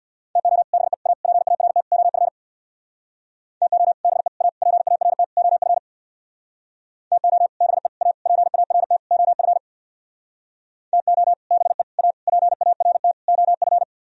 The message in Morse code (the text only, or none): TO BE A PART OF TO BE A PART OF TO BE A PART OF TO BE A PART OF